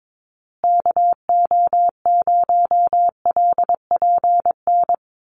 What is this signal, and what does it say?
Morse code, heard as a keyed tone, about 22 wpm, 700 Hz. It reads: KO0LPN
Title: USA Call Signs Repeated 3x 22wpm